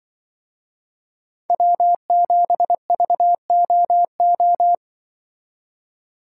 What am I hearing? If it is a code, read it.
Morse code, heard as a keyed tone, about 24 wpm, 700 Hz. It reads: W7VOO
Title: Speed Racing - USA 1x3 Call Signs 20wpm